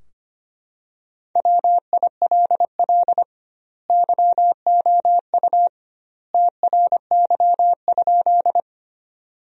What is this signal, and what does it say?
Morse code, heard as a keyed tone, about 25 wpm, 700 Hz. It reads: WILL YOU TRY?
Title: 2x Word-Spacing - Sentences from Top 200 Words - 25wpm